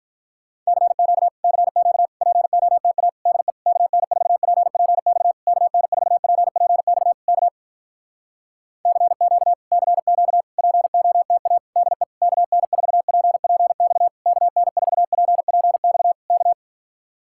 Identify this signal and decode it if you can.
Morse code, heard as a keyed tone, about 46 wpm, 700 Hz. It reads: CQ CQ POTA DE KN4PPX KN4PPX K CQ CQ POTA DE KN4PPX KN4PPX K